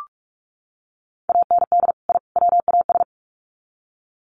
Morse code, heard as a keyed tone, about 44 wpm, 700 Hz. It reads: AND I WAS
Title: Sets of 3 Words 45wpm